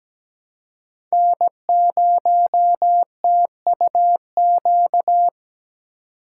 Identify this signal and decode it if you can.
Morse code, heard as a keyed tone, about 20 wpm, 700 Hz. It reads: N0TUQ
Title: Speed Racing - USA 1x3 Call Signs 17wpm